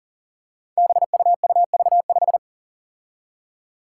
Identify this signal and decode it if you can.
Morse code, heard as a keyed tone, about 37 wpm, 700 Hz. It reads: BUUV5
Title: Five Challenging Letter-Number Combinations 40wpm